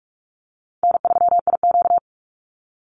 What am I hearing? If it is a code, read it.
Morse code, heard as a keyed tone, about 44 wpm, 700 Hz. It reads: N3IQ
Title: Easy Call Signs Repeated 3x - 45wpm